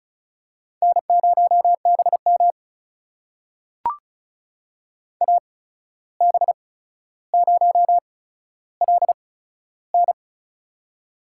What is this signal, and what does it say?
Morse code, heard as a keyed tone, about 36 wpm, spaced as for 14 wpm, 700 Hz. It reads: N0BM AB0LN
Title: USA Call Signs - 8x Character Spacing - 35wpm